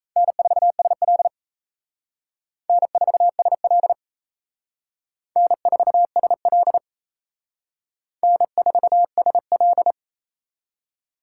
Morse code, heard as a keyed tone, about 34 wpm, 700 Hz. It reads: N4SL N4SL N4SL N4SL
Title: Speed Racing - USA 1x2 Call Signs 28wpm